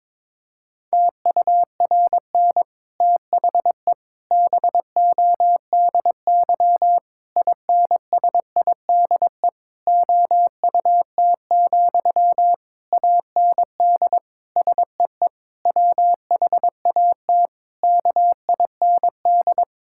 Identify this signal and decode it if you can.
Morse code, heard as a keyed tone, about 22 wpm, 700 Hz. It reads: TURN THE BODY INSIDE OUT, AND SEE WHAT KIND
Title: Sentences from Top 1000 Words - Review Part 6 - 22wpm